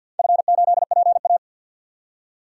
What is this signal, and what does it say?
Morse code, heard as a keyed tone, about 46 wpm, 700 Hz. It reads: F8PA
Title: DXCC-100 Call Signs - 4 Characters - 50wpm